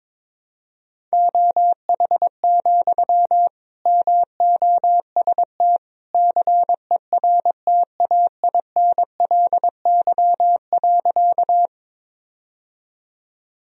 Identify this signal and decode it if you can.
Morse code, heard as a keyed tone, about 22 wpm, 700 Hz. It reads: OH, MOST CERTAINLY.
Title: Sentences from Top 1000 Words - Review Part 1 - 22wpm